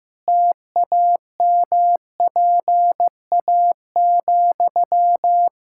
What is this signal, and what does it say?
Morse code, heard as a keyed tone, about 15 wpm, 700 Hz. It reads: TAMPA,
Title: Practice QSO - Exchange 1 - Beginning - 15wpm